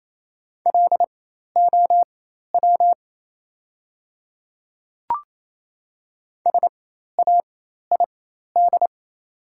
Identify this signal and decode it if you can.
Morse code, heard as a keyed tone, about 28 wpm, spaced as for 13 wpm, 700 Hz. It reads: LOW SAID
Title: 4x Character Spacing - Top 200 Words - 28wpm